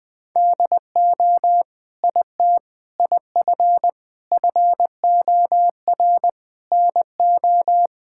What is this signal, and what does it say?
Morse code, heard as a keyed tone, about 20 wpm, 700 Hz. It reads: DO IT IF FOR NO
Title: Sentences with 7-Word Phrases - 20wpm